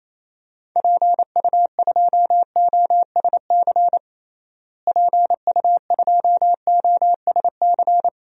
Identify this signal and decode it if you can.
Morse code, heard as a keyed tone, about 28 wpm, 700 Hz. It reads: PU2OSC PU2OSC
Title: DXCC-100 Call Signs - 6 Characters - Repeated 2x - 28wpm